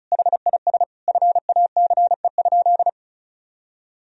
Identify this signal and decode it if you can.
Morse code, heard as a keyed tone, about 36 wpm, 700 Hz. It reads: HIS FACE?
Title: Sentences from Top 1000 Words - Review Part 2 - 35wpm